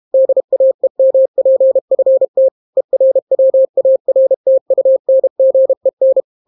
Morse code, heard as a keyed tone, about 32 wpm, 530 Hz. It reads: DAEMPFT ERWARTUNGEN